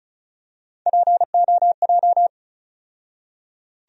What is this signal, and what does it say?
Morse code, heard as a keyed tone, about 36 wpm, 700 Hz. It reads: POJ
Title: USA Call Sign Postfixes - 3 Letters 35wpm